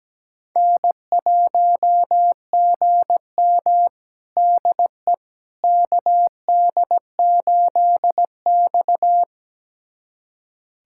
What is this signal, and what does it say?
Morse code, heard as a keyed tone, about 17 wpm, 700 Hz. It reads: N1GM DE KD8X
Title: USA Callsign DE Callsign - 4 Characters - 17wpm